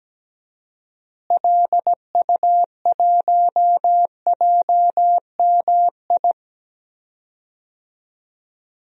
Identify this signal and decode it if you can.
Morse code, heard as a keyed tone, about 17 wpm, 700 Hz. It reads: LU1JMI
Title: DXCC-100 Call Signs - 6 Characters - Repeated 2x - 17wpm